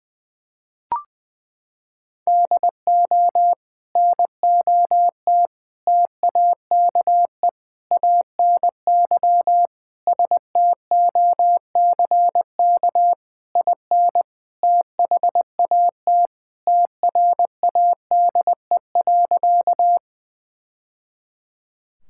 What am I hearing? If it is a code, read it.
Morse code, heard as a keyed tone, about 21 wpm, 700 Hz. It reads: DO NOT TAKE ANY STOCK IN THAT TRADE.
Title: Sentences from Top 1000 Words - Review Part 5 - 20wpm